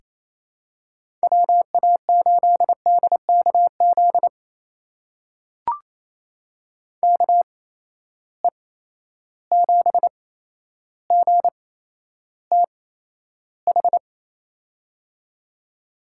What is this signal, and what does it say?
Morse code, heard as a keyed tone, about 28 wpm, spaced as for 12 wpm, 700 Hz. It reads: WA8DKZ KE7GTH
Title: USA Call Signs - 6 Characters - 8x Character Spacing - 28wpm